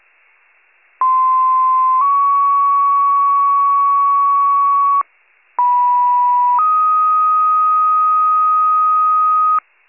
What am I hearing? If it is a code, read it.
Two-tone page, heard as a one-second tone, then a three-second tone, about 1040 then 1120 Hz, about 980 then 1250 Hz. It is heard from 1.0 to 9.6 s.
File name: pager / 95 RESCUE / wcalert